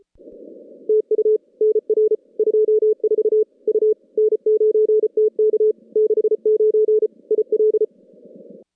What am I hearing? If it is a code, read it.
Morse code, heard as a keyed tone, about 31 wpm, 410 Hz. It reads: TU NR 24 U N9TK 69 IL
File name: contests / CW-SS-k5tr- -2100